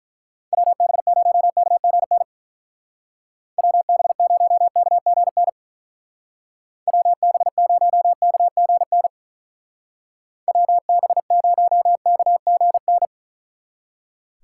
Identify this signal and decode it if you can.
Morse code, heard as a keyed tone, about 43 wpm, 700 Hz. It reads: WB0KGN WB0KGN WB0KGN WB0KGN
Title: Speed Racing - USA Call Signs 35wpm